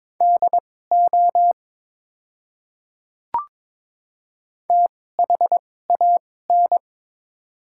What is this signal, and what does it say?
Morse code, heard as a keyed tone, about 22 wpm, spaced as for 16 wpm, 700 Hz. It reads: DO THAN
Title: 2x Character Spacing - Top 100 Words - 22wpm